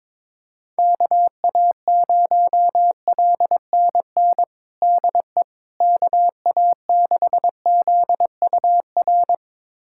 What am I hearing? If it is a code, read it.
Morse code, heard as a keyed tone, about 22 wpm, 700 Hz. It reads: KA0LNN DE KA6ZUR
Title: USA Callsign DE Callsign - 6 Characters - 22wpm